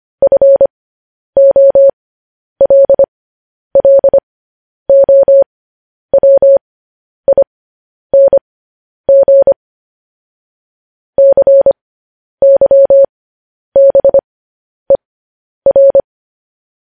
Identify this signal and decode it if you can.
Morse code, heard as a keyed tone, about 25 wpm, spaced as for 10 wpm, 550 Hz. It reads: FOLLOWING CYBER
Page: News Headlines in Morse Code at 10 WPM